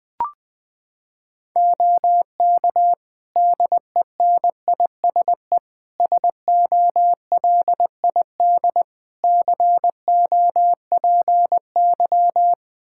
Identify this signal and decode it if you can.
Morse code, heard as a keyed tone, about 20 wpm, 700 Hz. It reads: OK DENISE SOLID COPY
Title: K1USN SST QSO Practice 20wpm